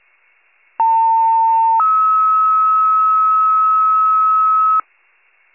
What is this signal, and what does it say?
Two-tone page, heard as a one-second tone, then a three-second tone, about 900 then 1290 Hz, with noise. It is heard from 0.8 to 4.8 s.